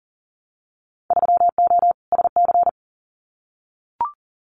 Morse code, heard as a keyed tone, about 40 wpm, 700 Hz. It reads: 3O SC
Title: Winter Field Day Exchanges Encore 40wpm